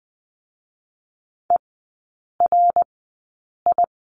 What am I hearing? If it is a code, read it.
Morse code, heard as a keyed tone, about 20 wpm, spaced as for 8 wpm, 700 Hz. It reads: ERI
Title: 053 Lesson 10e - Letter r - Three Characters 20wpm